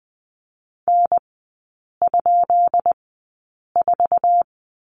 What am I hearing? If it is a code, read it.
Morse code, heard as a keyed tone, about 20 wpm, spaced as for 8 wpm, 700 Hz. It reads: N?4
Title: 133 Lesson 22e - Question Mark - Three Characters 20wpm